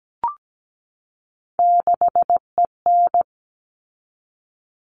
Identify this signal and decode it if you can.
Morse code, heard as a keyed tone, about 17 wpm, 700 Hz. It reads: 6EN
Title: All Cut Numbers - 3-Digit 17wpm